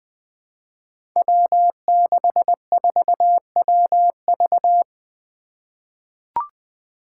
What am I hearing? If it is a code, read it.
Morse code, heard as a keyed tone, about 20 wpm, 700 Hz. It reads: W64WV
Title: Five Challenging Letter-Number Combinations 20wpm